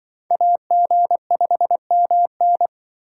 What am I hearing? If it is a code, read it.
Morse code, heard as a keyed tone, about 25 wpm, 700 Hz. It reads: AG5MN
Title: Speed Racing - USA 2x2 Call Signs 20wpm